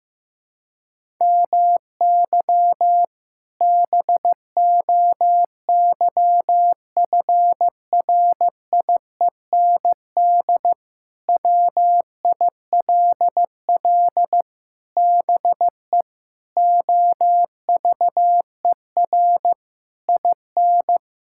Morse code, heard as a keyed tone, about 15 wpm, 700 Hz. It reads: MY BOYFRIEND WILL BE OVER IN
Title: Sentences from Top 1000 Words - Review Part 5 - 15wpm